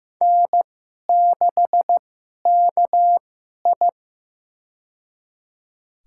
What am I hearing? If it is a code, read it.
Morse code, heard as a keyed tone, about 15 wpm, spaced as for 11 wpm, 700 Hz. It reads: N6KI
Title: USA Call Signs - 4 Characters - 2x Character Spacing - 15wpm